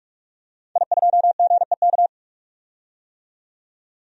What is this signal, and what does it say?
Morse code, heard as a keyed tone, about 44 wpm, 700 Hz. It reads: I2GEK
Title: DXCC-100 Call Signs - 5 Characters - 45wpm